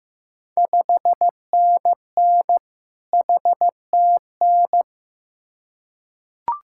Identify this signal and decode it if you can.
Morse code, heard as a keyed tone, about 15 wpm, 700 Hz. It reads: 5NN HTN
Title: Ontario QSO Party Exchange 15wpm